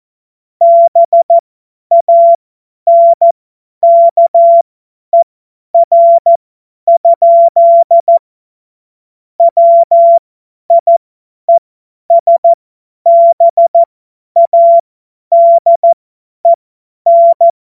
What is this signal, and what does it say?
Morse code, heard as a keyed tone, about 14 wpm, spaced as for 10 wpm, 680 Hz. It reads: BANKER? WIESBADEN
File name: Morsekurs / Witze / Witz 03 Juni 2025 50 F70